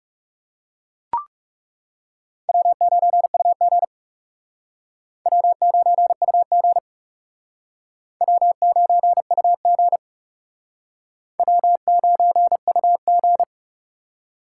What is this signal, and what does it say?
Morse code, heard as a keyed tone, about 37 wpm, 700 Hz. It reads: W9UG W9UG W9UG W9UG
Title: Speed Racing - USA Call Signs 30wpm